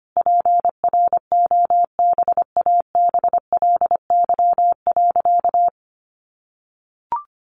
Morse code, heard as a keyed tone, about 25 wpm, 700 Hz. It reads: PROBABLY.
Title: Sentences from Top 700 Words 25wpm